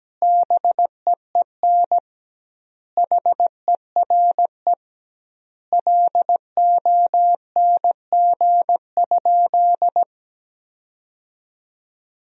Morse code, heard as a keyed tone, about 17 wpm, 700 Hz. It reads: BEEN HERE LONG?
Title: 2x Word-Spacing - Sentences from Top 200 Words - 17wpm